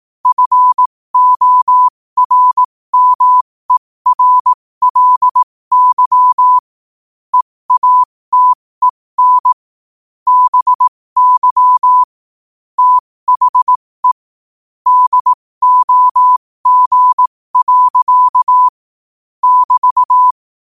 Morse code, heard as a keyed tone, about 18 wpm, spaced as for 15 wpm, 1000 Hz. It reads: FORMERLY EATEN BY THE DOG. <BT>
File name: qotdfiles / 15wpm